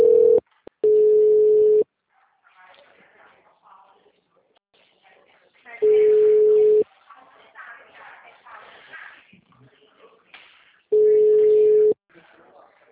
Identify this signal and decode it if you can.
Ringback tone (Europe, 425 Hz), with speech.